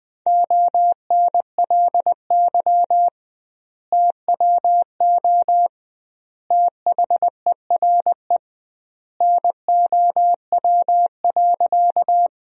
Morse code, heard as a keyed tone, about 20 wpm, 700 Hz. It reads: ONLY TWO THERE NOW.
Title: 2x Word-Spacing - Sentences from Top 200 Words - 20wpm